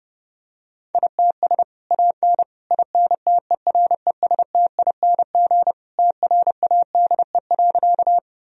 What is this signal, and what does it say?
Morse code, heard as a keyed tone, about 30 wpm, 700 Hz. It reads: ITS AN INTERESTING TRADE.
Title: Sentences from Top 1000 Words - Review Part 4 - 30wpm